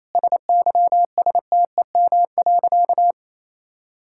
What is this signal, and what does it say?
Morse code, heard as a keyed tone, about 28 wpm, 700 Hz. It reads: SYSTEM.
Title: Sentences from Top 1000 Words - Review Part 6 - 28wpm